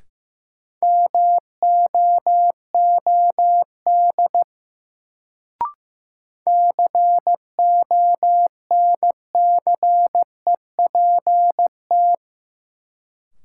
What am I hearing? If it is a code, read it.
Morse code, heard as a keyed tone, about 15 wpm, 700 Hz. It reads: MOOD CONCEPT
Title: Sentences from Top 1100 Words - New Word Drill Alt - 15wpm